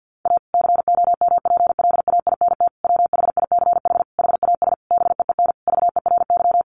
Morse code, heard as a keyed tone, about 50 wpm, 700 Hz. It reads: A COMPLAINT WHICH HAS BEEN VERY
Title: Sentences from Top 1400 Words 50wpm